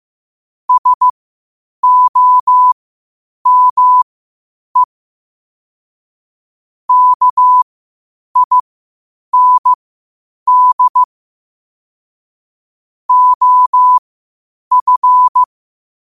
Morse code, heard as a keyed tone, about 15 wpm, spaced as for 8 wpm, 1000 Hz. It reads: SOME KIND OF